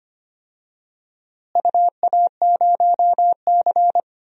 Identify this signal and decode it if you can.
Morse code, heard as a keyed tone, about 25 wpm, 700 Hz. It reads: UA0C
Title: DXCC-100 Call Signs - 4 Characters - Repeated 2x - 25wpm